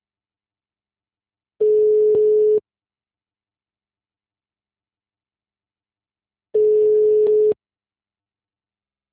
Ringback tone (Europe, 425 Hz).